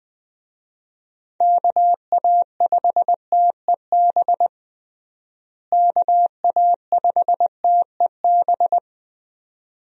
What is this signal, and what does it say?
Morse code, heard as a keyed tone, about 20 wpm, 700 Hz. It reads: KA5TEB KA5TEB
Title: USA Call Signs - 6 Characters - Repeated 2x - 20wpm